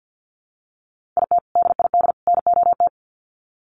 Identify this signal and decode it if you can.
Morse code, heard as a keyed tone, about 50 wpm, 700 Hz. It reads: IT DID NOT
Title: Sets of 3 Words 50wpm